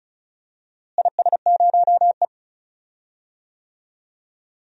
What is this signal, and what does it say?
Morse code, heard as a keyed tone, about 34 wpm, 700 Hz. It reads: IS0E
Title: DXCC-100 Call Signs - 4 Characters - Repeated 2x - 35wpm